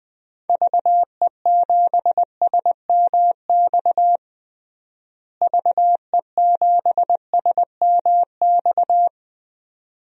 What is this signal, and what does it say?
Morse code, heard as a keyed tone, about 21 wpm, 700 Hz. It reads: VE7SMX VE7SMX
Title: DXCC-100 Call Signs - 6 Characters - Repeated 2x - 20wpm